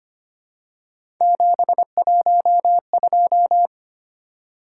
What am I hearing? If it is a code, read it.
Morse code, heard as a keyed tone, about 25 wpm, 700 Hz. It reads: 712
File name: Three Letter-Number Combinations - 25wpm